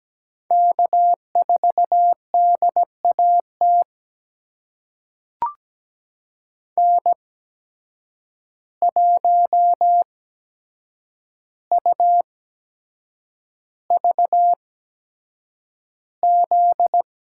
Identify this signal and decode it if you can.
Morse code, heard as a keyed tone, about 17 wpm, spaced as for 7 wpm, 700 Hz. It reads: K4DAT N1UVZ
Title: USA Call Signs - 5 Characters - 8x Character Spacing - 17wpm